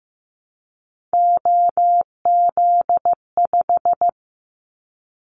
Morse code, heard as a keyed tone, about 15 wpm, 700 Hz. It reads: OZ5
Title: Callsign Prefixes for Top 50 DXCC - 15wpm